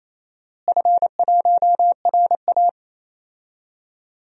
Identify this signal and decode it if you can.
Morse code, heard as a keyed tone, about 28 wpm, 700 Hz. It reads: F1RA
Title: DXCC-100 Call Signs - 4 Characters - Repeated 3x - 28wpm